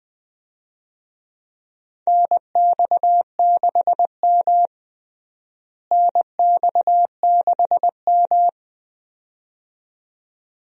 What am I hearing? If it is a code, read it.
Morse code, heard as a keyed tone, about 21 wpm, 700 Hz. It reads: NX6M NX6M
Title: USA Call Signs - 4 Characters - Repeated 2x - 20wpm